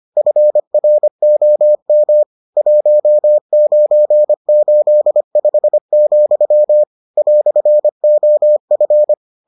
Morse code, heard as a keyed tone, about 25 wpm, 600 Hz. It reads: FROM 1985, "OF